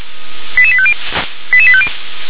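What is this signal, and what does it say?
Five-tone selective call (ZVEI), heard as quick runs of five tones, tones of about 70 ms. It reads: 88850 88850
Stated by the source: ZVEI five-tone selective call received by SDR.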